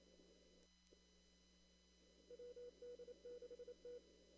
Morse code, heard as a keyed tone, about 27 wpm, 490 Hz. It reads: WD6T
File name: CW / kt5j wpx cw-op-k5pi